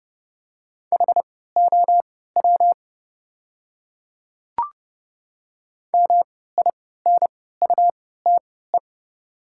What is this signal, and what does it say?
Morse code, heard as a keyed tone, about 30 wpm, spaced as for 17 wpm, 690 Hz. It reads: HOW MINUTE
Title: 3x Character Spacing - Top 500 Words - 30wpm